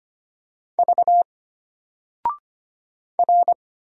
Morse code, heard as a keyed tone, about 25 wpm, 700 Hz. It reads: V R
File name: Single-Letter-Number-25wpm